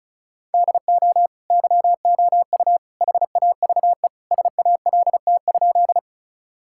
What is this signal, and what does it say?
Morse code, heard as a keyed tone, about 34 wpm, 700 Hz. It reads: DO YOU HAVE SALT?
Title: Sentences from Top 1000 Words - Review Part 5 - 35wpm